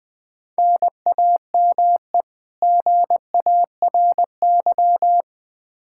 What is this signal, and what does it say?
Morse code, heard as a keyed tone, about 20 wpm, 700 Hz. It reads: NAME GARY
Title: Practice QSO - Exchange 1 - Beginning - 20wpm